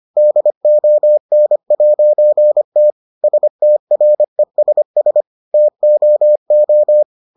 Morse code, heard as a keyed tone, about 25 wpm, 600 Hz. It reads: DON'T STRESS TOO